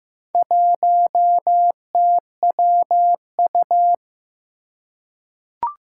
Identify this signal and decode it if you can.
Morse code, heard as a keyed tone, about 15 wpm, 700 Hz. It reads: 1TWU
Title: Four Challenging Letter-Number Combinations 15wpm